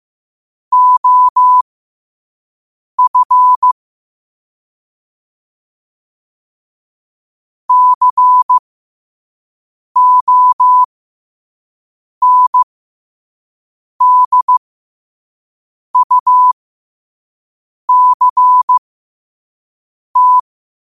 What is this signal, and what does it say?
Morse code, heard as a keyed tone, about 15 wpm, spaced as for 5 wpm, 1000 Hz. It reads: OF CONDUCT
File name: qotdfiles / 5wpm